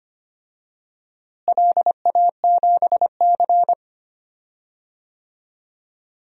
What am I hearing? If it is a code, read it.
Morse code, heard as a keyed tone, about 25 wpm, 700 Hz. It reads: LA7C
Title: DXCC-100 Call Signs - 4 Characters - Repeated 2x - 25wpm